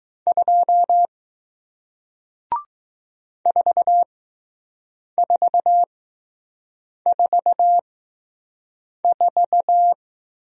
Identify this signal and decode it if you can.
Morse code, heard as a keyed tone, about 20 wpm, spaced as for 6 wpm, 700 Hz. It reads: 2 4444